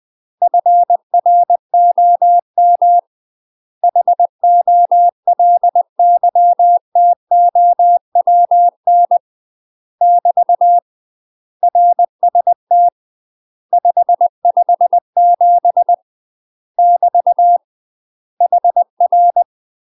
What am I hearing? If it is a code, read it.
Morse code, heard as a keyed tone, about 20 wpm, 700 Hz. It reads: FROM HOLYTOWN <BT> RST 557 <BT> HR
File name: UK MASTER QSOs TRAINING/20WPM